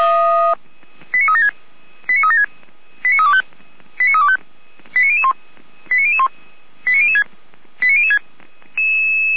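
Five-tone selective call (ZVEI), heard as quick runs of five tones, tones of about 70 ms. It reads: 89376 89376 89325 89325 89001 89001 89006 89006
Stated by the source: ZVEI five-tone selective call received on an SDR.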